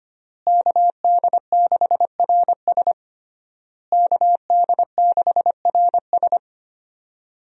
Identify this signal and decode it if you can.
Morse code, heard as a keyed tone, about 25 wpm, 700 Hz. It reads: KD6RS KD6RS
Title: USA Call Signs - 5 Characters - Repeated 3x - 25wpm